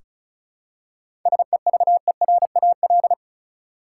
Morse code, heard as a keyed tone, about 36 wpm, 700 Hz. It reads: SEVERAL
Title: Seven Letter Words - 35wpm